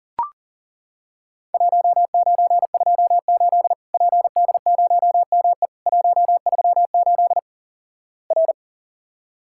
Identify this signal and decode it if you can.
Morse code, heard as a keyed tone, about 37 wpm, 700 Hz. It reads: 1928 PD0ME 138 R
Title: Worked All Europe DX Contest - QTC Exchange - 40wpm